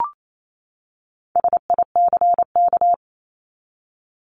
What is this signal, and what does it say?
Morse code, heard as a keyed tone, about 28 wpm, 700 Hz. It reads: SICK
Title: Top 1501-1750 Words 28wpm